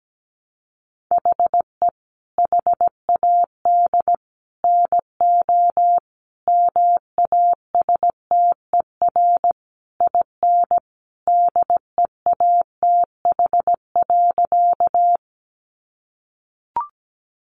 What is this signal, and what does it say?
Morse code, heard as a keyed tone, about 17 wpm, 700 Hz. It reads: HE HAD NO MASTER IN DEATH.
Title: Sentences from Top 1300 Words 17wpm